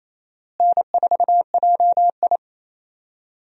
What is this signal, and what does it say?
Morse code, heard as a keyed tone, about 28 wpm, 700 Hz. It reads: N4JI
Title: USA Call Signs - 4 Characters - Repeated 2x - 28wpm